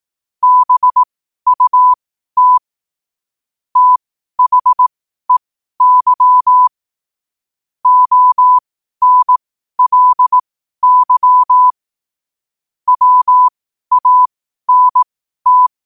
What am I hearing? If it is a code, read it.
Morse code, heard as a keyed tone, about 18 wpm, spaced as for 12 wpm, 1000 Hz. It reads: BUT THEY ONLY WANT
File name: qotdfiles / 12wpm